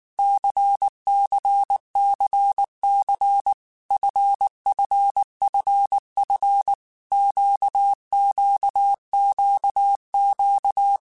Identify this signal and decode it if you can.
Morse code, heard as a keyed tone, about 19 wpm, 800 Hz. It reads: CCCC FFFF QQQQ
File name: Groups@18 / Groups@18 mp3 / LFCYQ@18